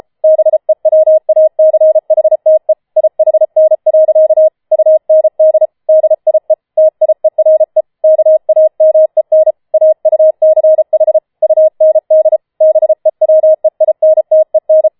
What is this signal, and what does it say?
Morse code, heard as a keyed tone, about 33 wpm, 630 Hz. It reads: BEWACHTE IHN. UND DIE TIERE KAMEN AUCH UND BEWEINTEN
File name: Schneewittchen Teil5 34wpm